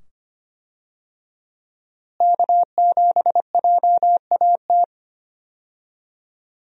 Morse code, heard as a keyed tone, about 25 wpm, 700 Hz. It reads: K7JAT
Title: USA Call Signs - 5 Characters - Repeated 2x - 25wpm